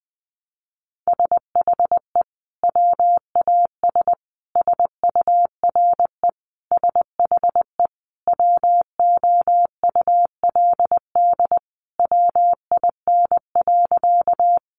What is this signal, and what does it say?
Morse code, heard as a keyed tone, about 20 wpm, 700 Hz. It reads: SHE WAS SURE SHE WOULD WIN.
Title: Sentences from Top 1500 Words 20wpm